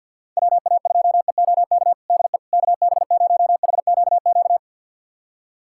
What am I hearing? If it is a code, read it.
Morse code, heard as a keyed tone, about 48 wpm, 700 Hz. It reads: WA2EOK DE KB0HQQ
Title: USA Callsign DE Callsign - 6 Characters - 50wpm